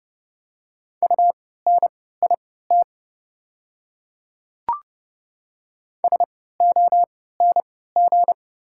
Morse code, heard as a keyed tone, about 30 wpm, spaced as for 17 wpm, 690 Hz. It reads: UNIT SONG
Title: 3x Character Spacing - Top 500 Words - 30wpm